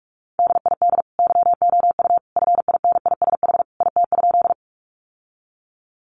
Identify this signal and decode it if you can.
Morse code, heard as a keyed tone, about 44 wpm, 700 Hz. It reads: DID YOU FINISH IT?
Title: Sentences from Top 1200 Words 45wpm